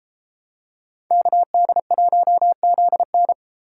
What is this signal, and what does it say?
Morse code, heard as a keyed tone, about 32 wpm, 700 Hz. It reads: KD1ZN